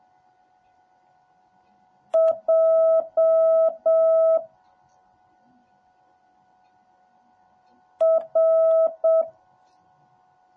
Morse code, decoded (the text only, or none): J R